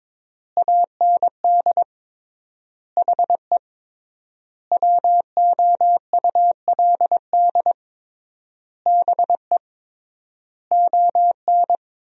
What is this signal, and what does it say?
Morse code, heard as a keyed tone, about 22 wpm, 700 Hz. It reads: AND HE WOULD BE ON
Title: 3x Word-Spacing - Sentences from Top 300 Words - 22wpm